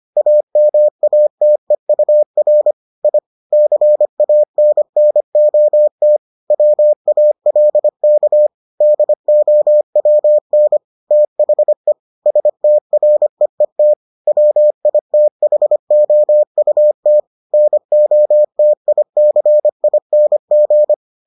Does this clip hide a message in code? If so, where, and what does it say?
Yes: Morse code from 0.2 to 20.9 s, heard as a keyed tone, about 25 wpm, 600 Hz. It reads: AMATEUR I CANNOT WALK DOWN THE STREET WITHOUT NOTICING